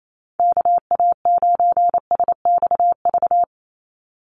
Morse code, heard as a keyed tone, about 28 wpm, 700 Hz. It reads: KA9SXV
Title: Call Signs - 28wpm